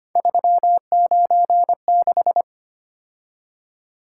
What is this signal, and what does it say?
Morse code, heard as a keyed tone, about 25 wpm, 700 Hz. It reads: 396
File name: Three Letter-Number Combinations - 25wpm